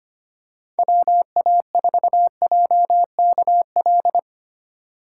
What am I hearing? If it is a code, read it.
Morse code, heard as a keyed tone, about 25 wpm, 700 Hz. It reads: WA4JKL
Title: K1USN SST QSO Practice - 25wpm